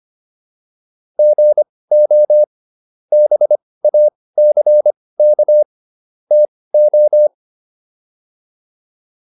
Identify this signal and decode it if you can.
Morse code, heard as a keyed tone, about 25 wpm, spaced as for 18 wpm, 600 Hz. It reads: GO BACK TO